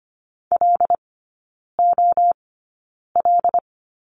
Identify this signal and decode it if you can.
Morse code, heard as a keyed tone, about 25 wpm, spaced as for 9 wpm, 690 Hz. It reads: LOL
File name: Lesson 14e - Number 2 - Three Characters 25wpm